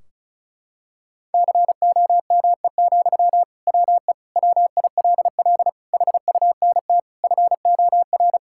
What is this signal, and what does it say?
Morse code, heard as a keyed tone, about 36 wpm, 700 Hz. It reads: COME, WE WILL HUNT FOR
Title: Sentences from Top 1600 Words 35wpm